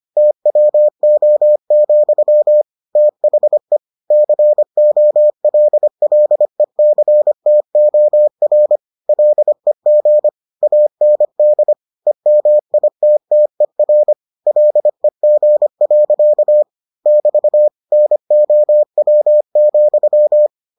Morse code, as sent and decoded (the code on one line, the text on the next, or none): - .-- --- --..-- / - .... / -.-. --- .-.. .-.. -.-. - --- .-. / .-.. --. / .- -. -.. / . -- .. - - . .-. / .-.. --. .-.-.- / -...- / -. --- .-- --..--
TWO, THE COLLECTOR LEG AND EMITTER LEG. <BT> NOW,